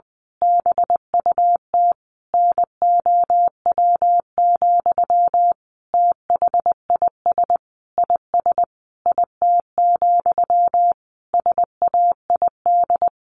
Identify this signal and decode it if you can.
Morse code, heard as a keyed tone, about 20 wpm, 700 Hz. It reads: BUT NOW, THIS IS IT, SAID
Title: A Sound Of Thunder 20wpm